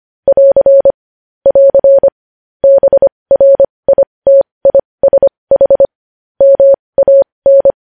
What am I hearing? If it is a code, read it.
Morse code, heard as a keyed tone, about 25 wpm, spaced as for 20 wpm, 550 Hz. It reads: <AR> <AR> BRITISH MAN
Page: news-headlines-in-morse-code-at-20-wpm